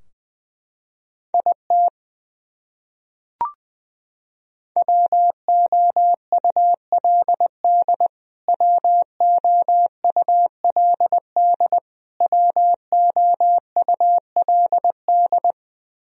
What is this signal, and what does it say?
Morse code, heard as a keyed tone, about 20 wpm, 700 Hz. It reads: IT WOULD WOULD WOULD
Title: Top 100 Words Repeated 20wpm